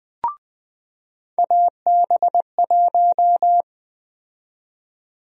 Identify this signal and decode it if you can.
Morse code, heard as a keyed tone, about 20 wpm, 700 Hz. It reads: AB1
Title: USA Call Sign Prefixes - 20wpm